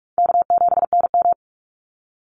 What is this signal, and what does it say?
Morse code, heard as a keyed tone, about 44 wpm, 700 Hz. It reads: K7NM
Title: Easy Call Signs - 45wpm